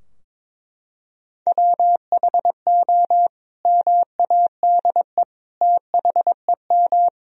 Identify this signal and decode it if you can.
Morse code, heard as a keyed tone, about 22 wpm, 700 Hz. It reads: WHO MADE THEM
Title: Sentences from Top 1000 Words - Review Part 2 - 22wpm